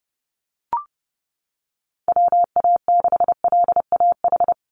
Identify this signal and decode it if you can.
Morse code, heard as a keyed tone, about 30 wpm, 700 Hz. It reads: WA6LAH
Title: USA Call Signs 30wpm